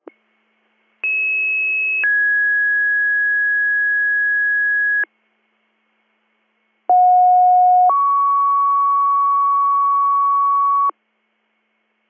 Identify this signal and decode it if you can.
Two-tone page, heard as a one-second tone, then a three-second tone, about 2600 then 1650 Hz, about 730 then 1120 Hz. It is heard from 1.0 to 10.9 s.